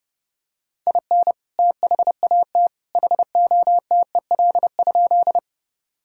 Morse code, heard as a keyed tone, about 32 wpm, 700 Hz. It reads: IN THAT HOTEL?
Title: Sentences from Top 1600 Words 30wpm